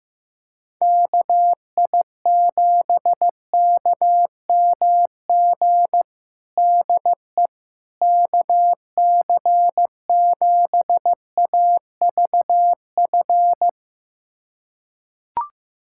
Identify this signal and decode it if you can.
Morse code, heard as a keyed tone, about 15 wpm, 700 Hz. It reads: KI7KMG DE KC7AVF
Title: USA Callsign DE Callsign - 6 Characters - 15wpm